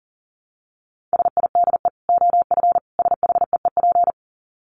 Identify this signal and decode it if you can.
Morse code, heard as a keyed tone, about 40 wpm, 700 Hz. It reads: SIDE OF SHEEP
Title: Dit Challenge 40wpm